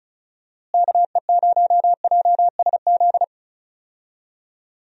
Morse code, heard as a keyed tone, about 36 wpm, 700 Hz. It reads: KE0JSZ
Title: USA Call Signs - 6 Characters - Repeated 2x - 35wpm